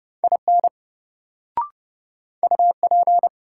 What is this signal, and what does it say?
Morse code, heard as a keyed tone, about 30 wpm, 700 Hz. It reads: IN UP
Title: Sentences from Top 100 Words - New Word Drill - 30wpm